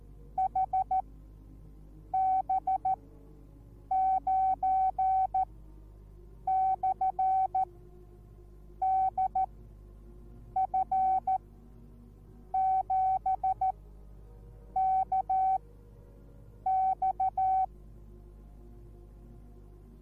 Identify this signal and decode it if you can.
Morse code, heard as a keyed tone, about 14 wpm, spaced as for 6 wpm, 760 Hz. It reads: HB9/DF7KX